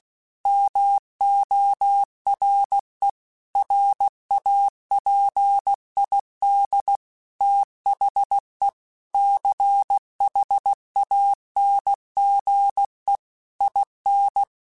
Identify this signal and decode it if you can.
Morse code, heard as a keyed tone, about 16 wpm, 800 Hz. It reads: MORE RAPID THE CHANGE IN